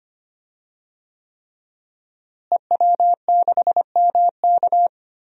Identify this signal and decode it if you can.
Morse code, heard as a keyed tone, about 25 wpm, 700 Hz. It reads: EW6MK
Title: DXCC-100 Call Signs - 5 Characters - Repeated 3x - 25wpm